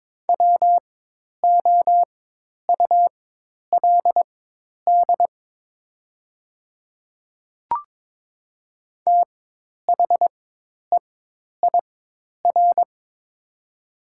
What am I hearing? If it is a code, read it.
Morse code, heard as a keyed tone, about 22 wpm, spaced as for 10 wpm, 700 Hz. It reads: WOULD THEIR
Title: 4x Character Spacing - Top 100 Words - 22wpm